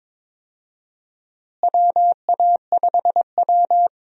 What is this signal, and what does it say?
Morse code, heard as a keyed tone, about 22 wpm, 700 Hz. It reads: WA5W
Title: USA Call Signs - 4 Characters - Repeated 2x - 22wpm